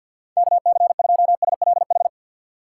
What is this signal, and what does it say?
Morse code, heard as a keyed tone, about 53 wpm, 690 Hz. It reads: KC2SLH
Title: USA Call Signs - 2x Character Spacing - 50wpm